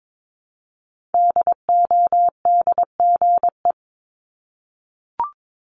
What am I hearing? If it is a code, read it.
Morse code, heard as a keyed tone, about 22 wpm, 700 Hz. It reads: DODGE
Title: Multi-letter Phonemes Repeated 3x 22wpm